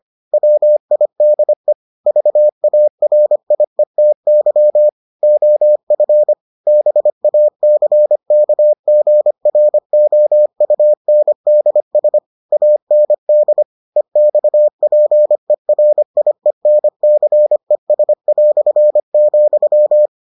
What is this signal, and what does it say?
Morse code, heard as a keyed tone, about 25 wpm, 600 Hz. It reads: WIDE VARIETY OF BACKGROUNDS AND EXPERIENCES",